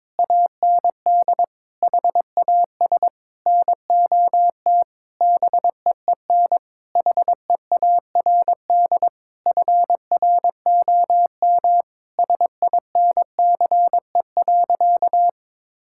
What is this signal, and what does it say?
Morse code, heard as a keyed tone, about 22 wpm, 700 Hz. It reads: AND HAS NOT BEEN HEARD FROM SINCE.
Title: Sentences from Top 1600 Words 22wpm